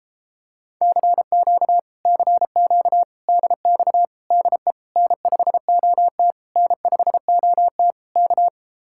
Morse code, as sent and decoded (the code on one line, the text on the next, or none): -.-. --.- / -.-. --.- / -.. -..- / -.. / -. ..... --- - / -. ..... --- - / -.-
CQ CQ DX DE N5OT N5OT K